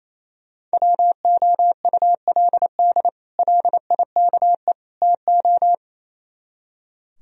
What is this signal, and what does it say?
Morse code, heard as a keyed tone, about 34 wpm, 700 Hz. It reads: WOULD LIKE TO
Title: Speed-Racing - Sets of 3 Words 28wpm